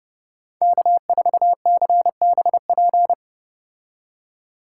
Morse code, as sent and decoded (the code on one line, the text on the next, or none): -.- ....- -.-. -... .--.
K4CBP